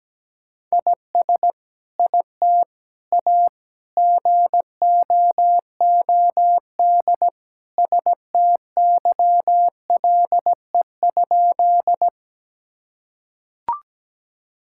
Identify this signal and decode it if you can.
Morse code, heard as a keyed tone, about 17 wpm, 700 Hz. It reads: IS IT A GOOD STYLE?
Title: Sentences from Top 1000 Words - Review Part 5 - 17wpm